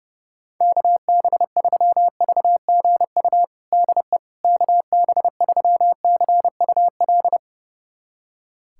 Morse code, heard as a keyed tone, about 30 wpm, 700 Hz. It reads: KB3VGU DE KB3CUL
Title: USA Callsign DE Callsign 30wpm